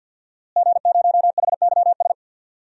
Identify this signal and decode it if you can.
Morse code, heard as a keyed tone, about 51 wpm, 700 Hz. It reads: G0HYS
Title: DXCC-100 Call Signs - 5 Characters - Repeated 3x - 50wpm